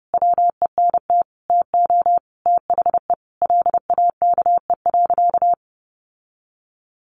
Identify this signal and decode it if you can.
Morse code, heard as a keyed tone, about 30 wpm, 700 Hz. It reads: WENT TO THE LAKE.
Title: Sentences from Top 1500 Words 30wpm